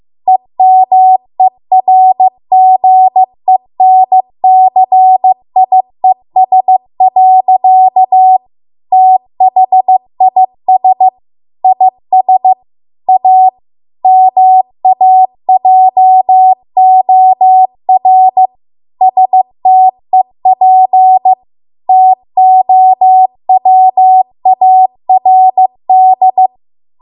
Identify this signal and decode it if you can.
Morse code, heard as a keyed tone, about 15 wpm, 750 Hz. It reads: EMERGENCIES. THIS IS A MAJOR STEP TOWARD